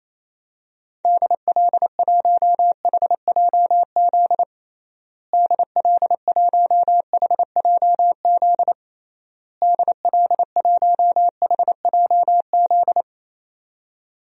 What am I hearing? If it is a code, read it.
Morse code, heard as a keyed tone, about 28 wpm, 700 Hz. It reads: DL1HJZ DL1HJZ DL1HJZ
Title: DXCC-100 Call Signs - 6 Characters - Repeated 3x - 28wpm